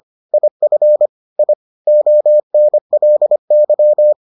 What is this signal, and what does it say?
Morse code, heard as a keyed tone, about 25 wpm, 600 Hz. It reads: IF I ONLY